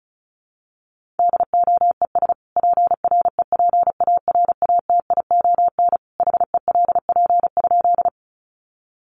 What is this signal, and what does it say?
Morse code, heard as a keyed tone, about 34 wpm, 700 Hz. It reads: DOES PREPARATION HELP?
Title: Sentences from Top 1400 Words 35wpm